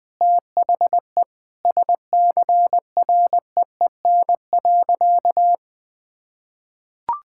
Morse code, heard as a keyed tone, about 20 wpm, 700 Hz. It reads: THE SCREEN.
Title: Sentences with 6-Word Phrases - Spoken Beginning - 20wpm